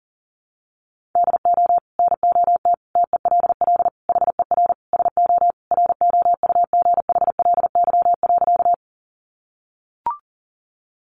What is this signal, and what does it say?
Morse code, heard as a keyed tone, about 37 wpm, 700 Hz. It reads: DO NOT TELL HER SO ROUGHLY.
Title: Sentences from Top 1500 Words 40wpm